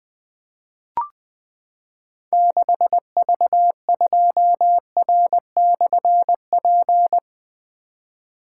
Morse code, heard as a keyed tone, about 20 wpm, 700 Hz. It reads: 6V2R/P
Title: Difficult Call Signs 20wpm